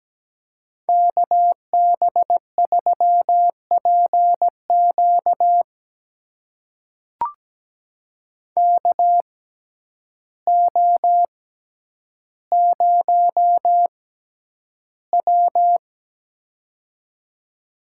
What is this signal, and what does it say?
Morse code, heard as a keyed tone, about 17 wpm, spaced as for 9 wpm, 700 Hz. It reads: KB3PQ KO0W
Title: USA Call Signs - 6x Character Spacing - 17wpm